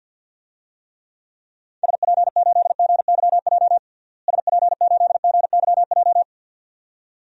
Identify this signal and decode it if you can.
Morse code, heard as a keyed tone, about 52 wpm, 700 Hz. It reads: SP8GYJ SP8GYJ
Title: DXCC-100 Call Signs - 6 Characters - Repeated 2x - 50wpm